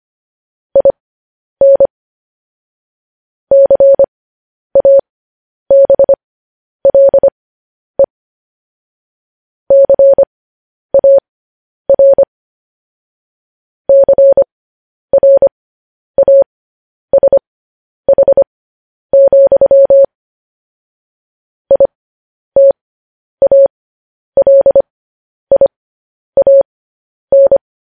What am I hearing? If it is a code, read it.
Morse code, heard as a keyed tone, about 25 wpm, spaced as for 10 wpm, 550 Hz. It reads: IN CABLE CAR CRASH, ITALIAN
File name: podcast / News headlines-1x-25x10